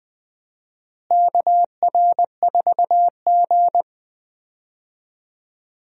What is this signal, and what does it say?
Morse code, heard as a keyed tone, about 21 wpm, 700 Hz. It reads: KR4G
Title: USA Call Signs - 4 Characters - Repeated 3x - 20wpm